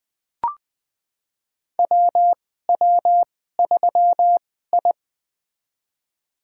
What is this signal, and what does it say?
Morse code, heard as a keyed tone, about 20 wpm, spaced as for 15 wpm, 700 Hz. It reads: WW3I
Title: USA Call Signs - 2x Character Spacing - 20wpm